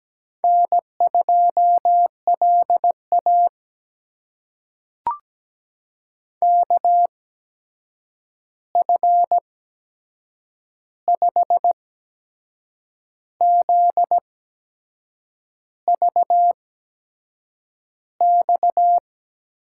Morse code, heard as a keyed tone, about 17 wpm, spaced as for 6 wpm, 700 Hz. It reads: N2LA KF5ZVX
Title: USA Call Signs - 8x Character Spacing - 17wpm